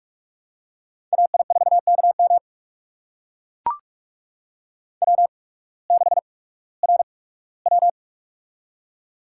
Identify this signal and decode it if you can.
Morse code, heard as a keyed tone, about 44 wpm, spaced as for 21 wpm, 700 Hz. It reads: AI4KM W6RW
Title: USA Call Signs - 8x Character Spacing - 45wpm